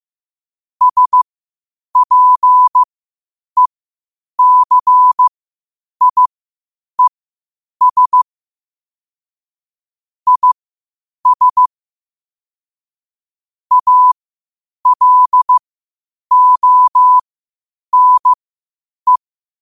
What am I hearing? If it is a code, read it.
Morse code, heard as a keyed tone, about 15 wpm, spaced as for 8 wpm, 1000 Hz. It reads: SPECIES IS ALONE
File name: qotdfiles / 8wpm